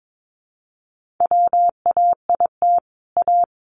Morse code, heard as a keyed tone, about 22 wpm, 700 Hz. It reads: WAIT A
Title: Sentences from Top 1000 Words 22wpm